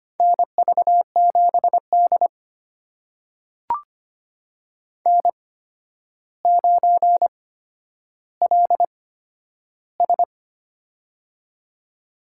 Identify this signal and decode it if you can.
Morse code, heard as a keyed tone, about 25 wpm, spaced as for 11 wpm, 700 Hz. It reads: NV7D N9LS
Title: USA Call Signs - 4 Characters - 8x Character Spacing - 25wpm